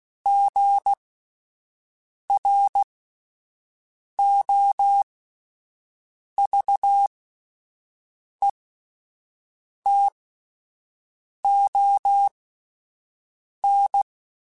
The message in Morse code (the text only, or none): GROVETON